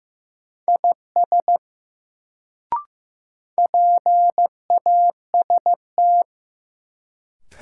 Morse code, heard as a keyed tone, about 15 wpm, 700 Hz. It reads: IS PAST
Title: Sets of 2 Words - New Word Drill - 15wpm